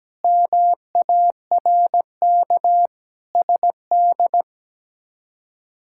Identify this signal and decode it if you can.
Morse code, heard as a keyed tone, about 17 wpm, 700 Hz. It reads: MARK SD
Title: K1USN SST QSO Practice - 17wpm